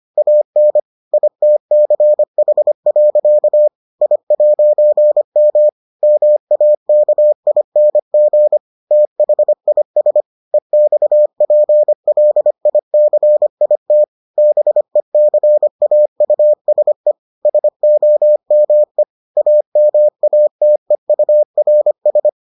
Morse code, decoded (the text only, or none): AN ITCH. I'M MAKING THIS EXPLICIT BECAUSE SOME AMATEURS